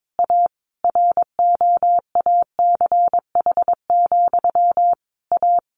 Morse code, heard as a keyed tone, about 22 wpm, 700 Hz. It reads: A ROACH, A
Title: A Sound Of Thunder 22wpm